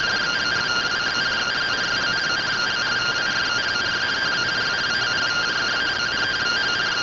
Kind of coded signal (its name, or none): two-tone teleprinter signal